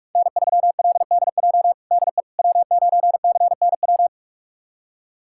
Morse code, heard as a keyed tone, about 43 wpm, 700 Hz. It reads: N3LDJ DE W9CNW
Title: USA Callsign DE Callsign - 5 Characters - 45wpm